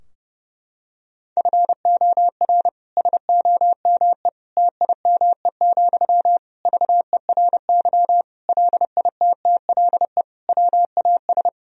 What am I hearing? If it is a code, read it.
Morse code, heard as a keyed tone, about 32 wpm, 700 Hz. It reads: FOR SOME TIME, VERY LITTLE WAS
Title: Sentences from Top 1000 Words - Review Part 4 - 30wpm